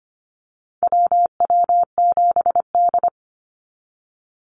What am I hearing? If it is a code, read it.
Morse code, heard as a keyed tone, about 25 wpm, 700 Hz. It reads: WW7D
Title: NA SOTA Activator Callsigns 25wpm